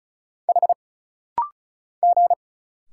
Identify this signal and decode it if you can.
Morse code, heard as a keyed tone, about 36 wpm, 700 Hz. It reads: H G
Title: Single Letter-Number Alt - Rapid-Fire 35wpm